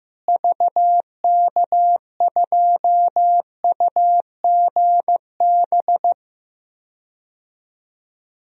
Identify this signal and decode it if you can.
Morse code, heard as a keyed tone, about 15 wpm, 700 Hz. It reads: VK2UGB
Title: DXCC-100 Call Signs - 6 Characters - Repeated 3x - 15wpm